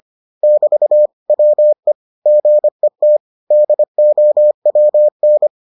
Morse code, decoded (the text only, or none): <BT> WE GET DOWN